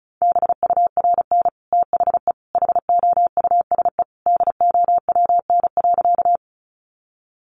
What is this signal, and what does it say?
Morse code, heard as a keyed tone, about 34 wpm, 700 Hz. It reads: BURN THE HOUSE DOWN.
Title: Sentences from Top 1300 Words 35wpm